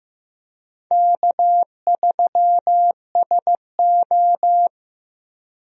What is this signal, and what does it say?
Morse code, heard as a keyed tone, about 15 wpm, 700 Hz. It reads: K3SO
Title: USA Call Signs - 4 Characters - Repeated 3x - 15wpm